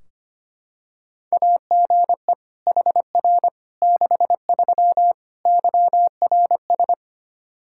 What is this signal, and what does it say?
Morse code, heard as a keyed tone, about 25 wpm, 700 Hz. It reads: AGE HR 63 YRS
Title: Practice QSO - Exchange 3 - More Info - 25wpm